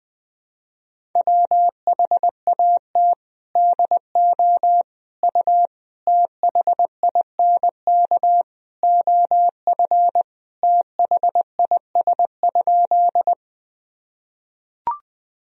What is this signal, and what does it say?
Morse code, heard as a keyed tone, about 20 wpm, 700 Hz. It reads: WHAT DO U THINK OF THIS?
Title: Sentences from Top 200 Words and Abbreviations 20wpm